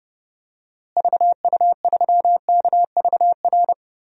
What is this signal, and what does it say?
Morse code, heard as a keyed tone, about 30 wpm, 700 Hz. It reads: VU3KVR
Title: DXCC-100 Call Signs - 6 Characters - Repeated 3x - 30wpm